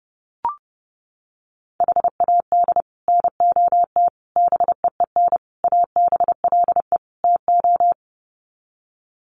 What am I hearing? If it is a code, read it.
Morse code, heard as a keyed tone, about 30 wpm, 700 Hz. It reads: HAD NOT BEEN ABLE TO
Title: Sets of 5 Words Encore 30wpm